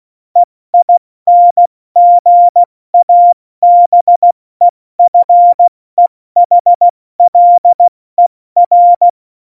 Morse code, heard as a keyed tone, about 16 wpm, 700 Hz. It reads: EINGABEFEHLER
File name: Morsekurs / Witze / Witz 22. Januar 2019, 70 F80